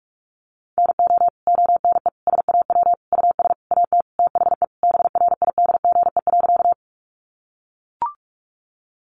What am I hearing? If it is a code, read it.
Morse code, heard as a keyed tone, about 44 wpm, 700 Hz. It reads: NO ONE SAW US AT THE BRIDGE.
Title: Sentences from Top 1200 Words 45wpm